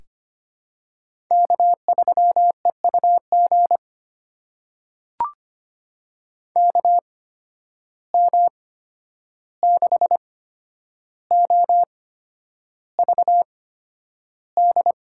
Morse code, heard as a keyed tone, about 25 wpm, spaced as for 10 wpm, 700 Hz. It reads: K3EUG KM6OVD
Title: USA Call Signs - 8x Character Spacing - 25wpm